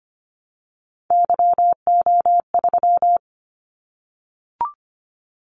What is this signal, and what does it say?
Morse code, heard as a keyed tone, about 25 wpm, 700 Hz. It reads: YO3
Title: Callsign Prefixes for Top 25 DXCC - 25wpm